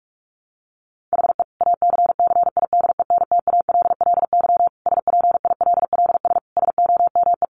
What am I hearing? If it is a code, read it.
Morse code, heard as a keyed tone, about 44 wpm, 700 Hz. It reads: HE ACCIDENTALLY SPILLS SOME
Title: Additional Easy Phrases 45wpm